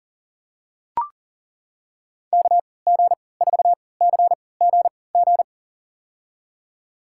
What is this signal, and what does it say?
Morse code, heard as a keyed tone, about 40 wpm, spaced as for 23 wpm, 700 Hz. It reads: KG4CGG
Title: USA Call Signs - 3x Character Spacing - 40wpm